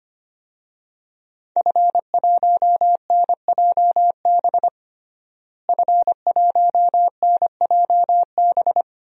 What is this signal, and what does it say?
Morse code, heard as a keyed tone, about 25 wpm, 700 Hz. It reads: F1NJB F1NJB
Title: DXCC-100 Call Signs - 5 Characters - Repeated 2x - 25wpm